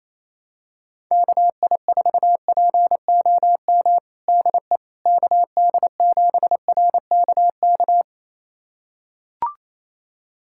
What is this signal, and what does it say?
Morse code, heard as a keyed tone, about 28 wpm, 700 Hz. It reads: KI4POM DE KD7RKK
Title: USA Callsign DE Callsign - 6 Characters - 28wpm